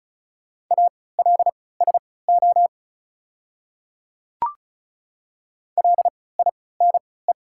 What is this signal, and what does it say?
Morse code, heard as a keyed tone, about 36 wpm, spaced as for 20 wpm, 700 Hz. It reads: ALSO LINE
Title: 3x Character Spacing - Top 200 Words - 35wpm